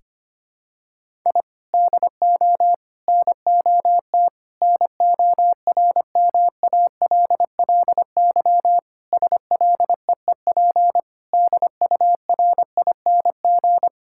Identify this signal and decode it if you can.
Morse code, heard as a keyed tone, about 25 wpm, 700 Hz. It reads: I DO NOT NORMALLY SLEEP DURING
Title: Sentences from Top 1000 Words - Review Part 6 - 25wpm